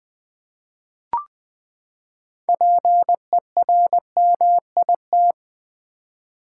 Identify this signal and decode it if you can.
Morse code, heard as a keyed tone, about 21 wpm, 700 Hz. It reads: PERMIT
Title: Six Letter Words - 20wpm